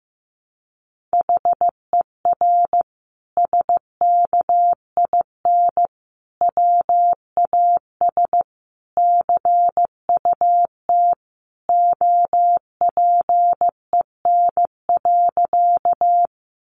Morse code, heard as a keyed tone, about 15 wpm, 700 Hz. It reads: HER SKIN WAS CUT OPEN.
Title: Sentences from Top 1300 Words 15wpm